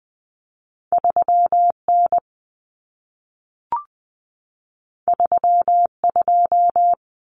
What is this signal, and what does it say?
Morse code, heard as a keyed tone, about 20 wpm, 700 Hz. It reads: 3N 32
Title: Common Cut Numbers - 2-Digit 20wpm